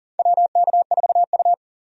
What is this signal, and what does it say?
Morse code, heard as a keyed tone, about 40 wpm, 700 Hz. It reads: WK4U
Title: USA Call Signs - 4 Characters - 40wpm